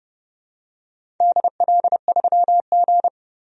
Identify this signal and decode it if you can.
Morse code, heard as a keyed tone, about 30 wpm, 700 Hz. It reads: DL3G
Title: DXCC-100 Call Signs - 4 Characters - 30wpm